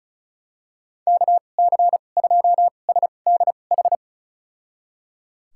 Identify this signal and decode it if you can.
Morse code, heard as a keyed tone, about 36 wpm, spaced as for 25 wpm, 700 Hz. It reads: KC2SDH
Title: USA Call Signs - 6 Characters - 2x Character Spacing - 35wpm